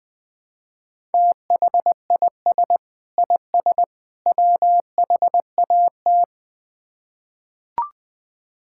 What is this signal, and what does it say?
Morse code, heard as a keyed tone, about 20 wpm, 700 Hz. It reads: THIS IS WHAT
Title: Build A Phrase - 3-Word Phrases - 20wpm